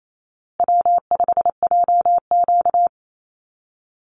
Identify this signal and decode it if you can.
Morse code, heard as a keyed tone, about 28 wpm, 700 Hz. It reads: W5JQ
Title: Call Signs Repeated 3x - 28wpm